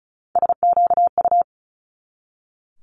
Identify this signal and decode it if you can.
Morse code, heard as a keyed tone, about 36 wpm, 700 Hz. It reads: SQU
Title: Multi-letter Phonemes 35wpm